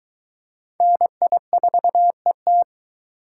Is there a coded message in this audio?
Yes: Morse code from 0.8 to 2.6 s, heard as a keyed tone, about 22 wpm, 700 Hz. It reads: NI4ET